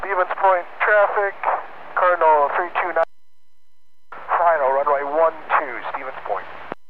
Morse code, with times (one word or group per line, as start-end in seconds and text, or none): none